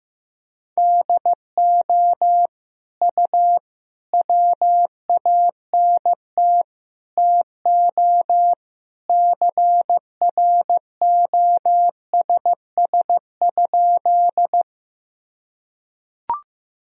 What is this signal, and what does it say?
Morse code, heard as a keyed tone, about 15 wpm, 700 Hz. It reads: DO U WANT TO CROSS?
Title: Sentences from Top 300 Words and Abbreviations 15wpm